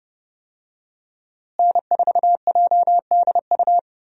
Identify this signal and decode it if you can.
Morse code, heard as a keyed tone, about 30 wpm, 700 Hz. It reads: N4JDU
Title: USA Call Signs - 5 Characters - Repeated 2x - 30wpm